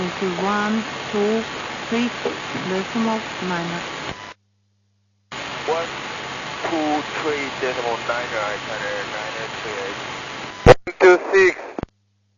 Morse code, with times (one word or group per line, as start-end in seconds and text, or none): none